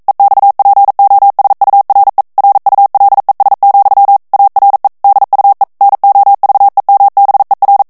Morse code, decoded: EXPOSURE RULES, ARE DUE NOVEMBER